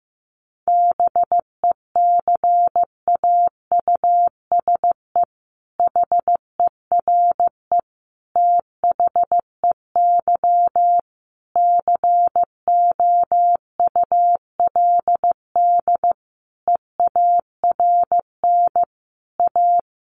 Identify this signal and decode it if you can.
Morse code, heard as a keyed tone, about 14 wpm, 700 Hz. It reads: BECAUSE HERE THEY COULD EARN A
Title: Sentences from Top 1500 Words 15wpm